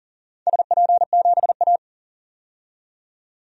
Morse code, heard as a keyed tone, about 37 wpm, 700 Hz. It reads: SP7A
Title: DXCC-100 Call Signs - 4 Characters - 40wpm